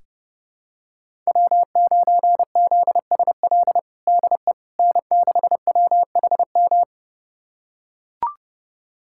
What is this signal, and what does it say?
Morse code, heard as a keyed tone, about 30 wpm, 700 Hz. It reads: W9ZSL DE N6WHM
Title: USA Callsign DE Callsign - 5 Characters - 30wpm